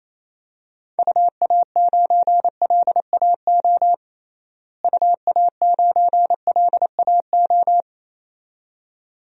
Morse code, heard as a keyed tone, about 27 wpm, 700 Hz. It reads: UA9LAO UA9LAO
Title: DXCC-100 Call Signs - 6 Characters - Repeated 2x - 28wpm